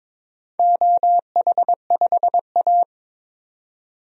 Morse code, heard as a keyed tone, about 22 wpm, 700 Hz. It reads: OH5A
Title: DXCC-100 Call Signs - 4 Characters - 22wpm